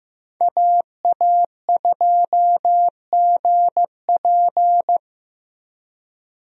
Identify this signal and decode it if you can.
Morse code, heard as a keyed tone, about 15 wpm, 700 Hz. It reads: AA2GP
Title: USA Call Signs - 5 Characters - 15wpm